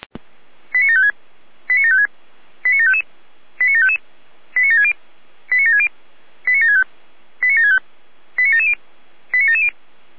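Five-tone selective call (ZVEI), heard as quick runs of five tones, tones of about 80 ms. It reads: 89756 89756 89755 89755 89760 89760 89765 89765 89770 89770
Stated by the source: ZVEI five-tone selective call 89750 received on SDR.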